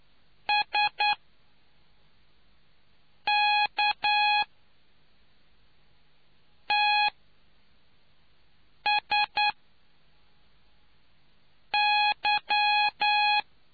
Morse code, heard as a keyed tone, about 9 wpm, spaced as for 3 wpm, 830 Hz. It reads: SKTSY